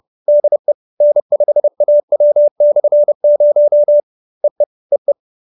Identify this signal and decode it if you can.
Morse code, heard as a keyed tone, about 30 wpm, 590 Hz. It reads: DE N5AW/0 EE EE